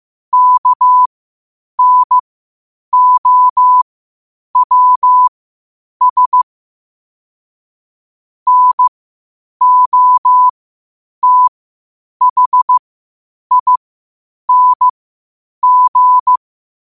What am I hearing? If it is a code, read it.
Morse code, heard as a keyed tone, about 15 wpm, spaced as for 8 wpm, 1000 Hz. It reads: KNOWS NOTHING
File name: qotdfiles / 8wpm